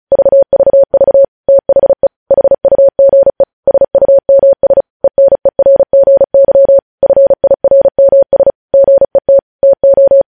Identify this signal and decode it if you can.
Morse code, heard as a keyed tone, about 36 wpm, 550 Hz. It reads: VVV THE HUGE SUMS ENERGY FIRMS GET TO
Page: news headlines in morse code at 35 wpm